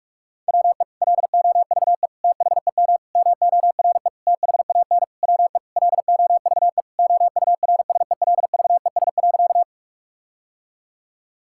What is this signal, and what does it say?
Morse code, heard as a keyed tone, about 46 wpm, 700 Hz. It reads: WE LOVE THEM MORE THAN WE LOVE OURSELVES.
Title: Sentences from Top 1600 Words 45wpm